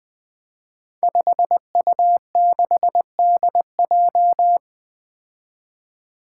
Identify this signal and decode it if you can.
Morse code, heard as a keyed tone, about 20 wpm, 700 Hz. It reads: 5U6DJ
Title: Five Challenging Letter-Number Combinations 20wpm